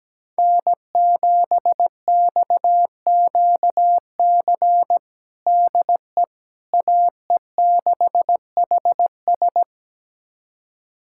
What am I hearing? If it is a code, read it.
Morse code, heard as a keyed tone, about 17 wpm, 700 Hz. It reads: N7XQC DE AE6HS
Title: USA Callsign DE Callsign - 5 Characters - 17wpm